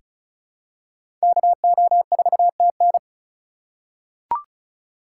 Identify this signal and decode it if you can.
Morse code, heard as a keyed tone, about 36 wpm, 700 Hz. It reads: KO4TN
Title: USA Call Signs - 5 Characters - 2x Character Spacing - 35wpm